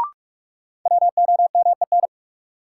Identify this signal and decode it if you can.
Morse code, heard as a keyed tone, about 41 wpm, 700 Hz. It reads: WOMEN
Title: Sentences from Top 1300 Words - New Word Drill Alt - 45wpm